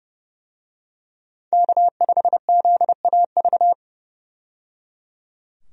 Morse code, heard as a keyed tone, about 30 wpm, 700 Hz. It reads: K5ZAV
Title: USA Call Signs - 5 Characters - 30wpm